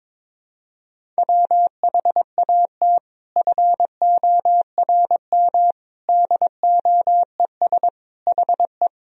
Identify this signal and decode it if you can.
Morse code, heard as a keyed tone, about 22 wpm, 700 Hz. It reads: WHAT FORM DOES HE
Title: Sentences from Top 1000 Words - Review Part 1 - 22wpm